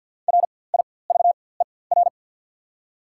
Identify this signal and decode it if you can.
Morse code, heard as a keyed tone, about 54 wpm, spaced as for 24 wpm, 700 Hz. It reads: RIVER
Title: 4x Character Spacing - Top 300 Words - 50wpm